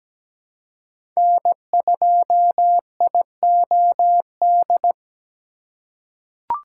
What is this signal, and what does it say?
Morse code, heard as a keyed tone, about 17 wpm, 700 Hz. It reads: N2IOD
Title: USA Call Signs - 5 Characters - 2x Character Spacing - 17wpm